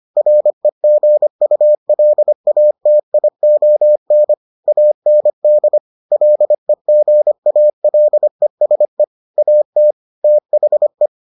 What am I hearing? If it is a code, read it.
Morse code, heard as a keyed tone, about 25 wpm, 600 Hz. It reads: REGULATION AND LEGALESE AT THE